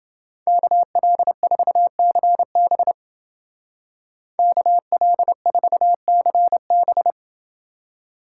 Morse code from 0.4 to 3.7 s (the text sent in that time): KL4CB